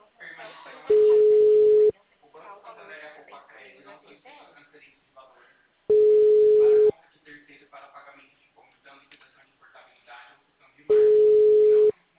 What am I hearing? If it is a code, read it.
Ringback tone (Europe, 425 Hz), with speech.